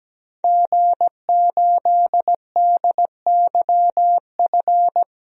Morse code, heard as a keyed tone, about 17 wpm, 700 Hz. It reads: G8DYF
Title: DXCC-100 Call Signs - 5 Characters - 17wpm